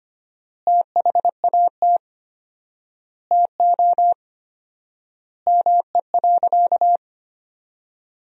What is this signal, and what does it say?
Morse code, heard as a keyed tone, about 25 wpm, 700 Hz. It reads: THAT TO ME.
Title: 4x Word-Spacing - Sentences from Top 400 Words - 25wpm